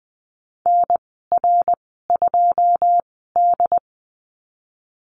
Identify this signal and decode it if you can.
Morse code, heard as a keyed tone, about 20 wpm, spaced as for 15 wpm, 700 Hz. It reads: NR2D
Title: 193 Lesson 31g - Letter v - Call Signs 20wpm